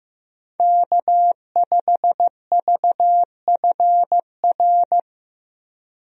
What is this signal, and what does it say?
Morse code, heard as a keyed tone, about 16 wpm, 700 Hz. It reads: K5VFR
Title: USA Call Signs - 5 Characters - Repeated 2x - 15wpm